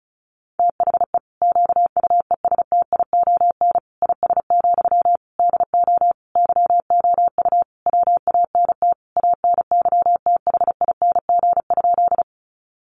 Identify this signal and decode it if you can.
Morse code, heard as a keyed tone, about 34 wpm, 700 Hz. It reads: THE QUESTION IS, DO YOU WANT ANYTHING?
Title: Sentences from Top 1000 Words 35wpm